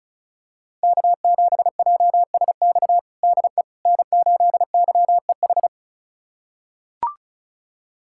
Morse code, heard as a keyed tone, about 36 wpm, 700 Hz. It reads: K7JSX DE N8YEH
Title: USA Callsign DE Callsign - 5 Characters - 35wpm